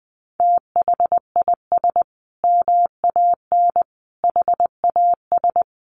Morse code, heard as a keyed tone, about 20 wpm, 700 Hz. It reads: THIS MAN HAS
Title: Sentences from Top 1300 Words 20wpm